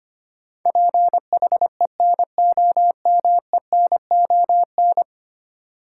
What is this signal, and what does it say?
Morse code, heard as a keyed tone, about 25 wpm, 700 Hz. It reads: PHENOMENON
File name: 10 Letter Words - First 3 Letters Given - 25wpm